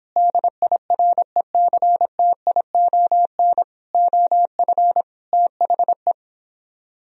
Morse code, heard as a keyed tone, about 31 wpm, 700 Hz. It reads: DIRECTION OF THE